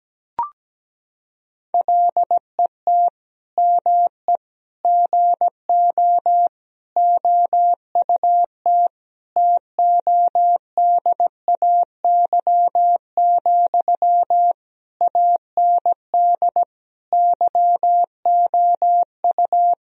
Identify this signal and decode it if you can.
Morse code, heard as a keyed tone, about 17 wpm, 700 Hz. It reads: LET ME GO OUT TODAY, AND YOU
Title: Sentences from Top 1000 Words - Review Part 4 - 17wpm